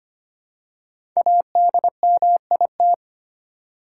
Morse code, heard as a keyed tone, about 25 wpm, 700 Hz. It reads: ADMIT
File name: 5 Letter Words - First 3 Letters Given - 25wpm